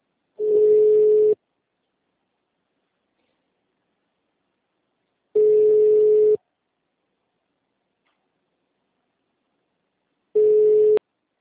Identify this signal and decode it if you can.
Ringback tone (Europe, 425 Hz).